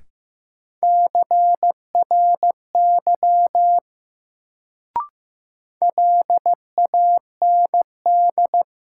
Morse code, heard as a keyed tone, about 15 wpm, 700 Hz. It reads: CRY LAND
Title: Sentences from Top 500 Words - New Word Drill - 15wpm